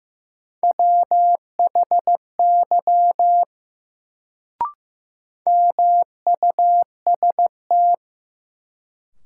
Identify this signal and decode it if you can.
Morse code, heard as a keyed tone, about 15 wpm, 700 Hz. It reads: WHY MUST
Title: Sentences from Top 200 Words - New Word Drill - 15wpm